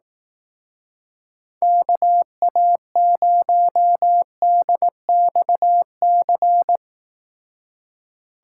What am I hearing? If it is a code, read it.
Morse code, heard as a keyed tone, about 18 wpm, 700 Hz. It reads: KA0DXC
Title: Speed Racing - USA Call Signs 15wpm